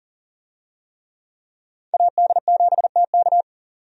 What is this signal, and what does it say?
Morse code, heard as a keyed tone, about 40 wpm, 700 Hz. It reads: AD7TK
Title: USA Call Signs - 5 Characters - 40wpm